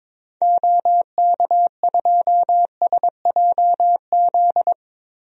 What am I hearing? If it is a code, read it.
Morse code, heard as a keyed tone, about 22 wpm, 700 Hz. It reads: OK2SJZ
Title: DXCC-100 Call Signs - 6 Characters - 22wpm